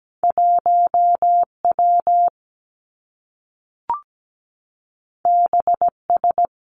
Morse code, heard as a keyed tone, about 17 wpm, 700 Hz. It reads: 1 W B S
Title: Two Challenging Letter-Number Combinations 17wpm